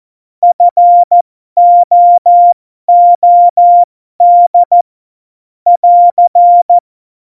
Morse code, heard as a keyed tone, about 14 wpm, spaced as for 12 wpm, 700 Hz. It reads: FOOD <AR>
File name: Morsekurs / Witze